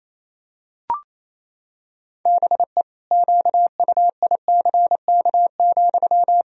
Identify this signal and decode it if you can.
Morse code, heard as a keyed tone, about 28 wpm, 700 Hz. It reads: BE QUICK,
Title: Sentences from Top 1000 Words - Review Part 2 - 28wpm